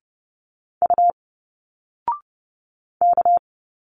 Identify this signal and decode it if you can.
Morse code, heard as a keyed tone, about 30 wpm, 700 Hz. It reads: U K
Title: Single Letters 30wpm